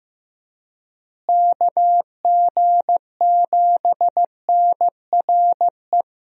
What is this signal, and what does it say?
Morse code, heard as a keyed tone, about 14 wpm, 700 Hz. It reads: KG7NRE
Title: USA Call Signs - 6 Characters - Repeated 3x - 15wpm